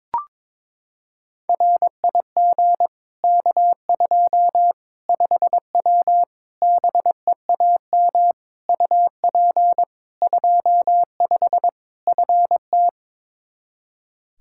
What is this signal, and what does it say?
Morse code, heard as a keyed tone, about 22 wpm, 700 Hz. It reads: RIG K2 5W BEAM UP 25 FT
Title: Practice QSO - Exchange 2 - Info - 22wpm